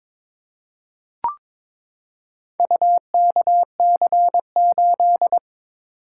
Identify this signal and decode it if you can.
Morse code, heard as a keyed tone, about 22 wpm, 700 Hz. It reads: UKC8
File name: practice / 22wpm / Four Letter-Number Combinations - 22wpm